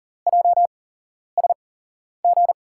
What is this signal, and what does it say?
Morse code, heard as a keyed tone, about 39 wpm, spaced as for 11 wpm, 700 Hz. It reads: JSG
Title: USA Call Signs - 5 Characters - 8x Character Spacing - 40wpm